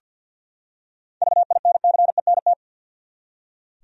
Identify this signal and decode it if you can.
Morse code, heard as a keyed tone, about 46 wpm, 700 Hz. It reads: VINCENT
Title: USA Common Names - 50wpm